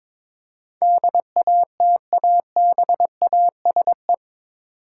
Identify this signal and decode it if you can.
Morse code, heard as a keyed tone, about 22 wpm, 700 Hz. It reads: DATABASE